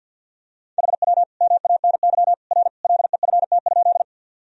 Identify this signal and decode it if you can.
Morse code, heard as a keyed tone, about 52 wpm, 700 Hz. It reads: HW MANY R LEFT?
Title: Sentences from Top 300 Words and Abbreviations 50wpm